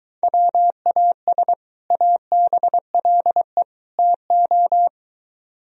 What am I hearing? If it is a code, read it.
Morse code, heard as a keyed tone, about 23 wpm, 700 Hz. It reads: WAS ABLE TO